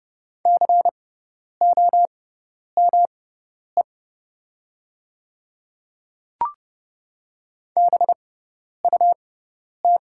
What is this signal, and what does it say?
Morse code, heard as a keyed tone, about 30 wpm, spaced as for 10 wpm, 700 Hz. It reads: COME BUT
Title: 6x Character Spacing - Top 200 Words - 30wpm